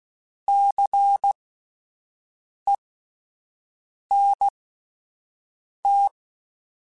Morse code, heard as a keyed tone, about 16 wpm, spaced as for 5 wpm, 800 Hz. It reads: CENT
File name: Alphabetical / Alphabetical mp3s / Cs @5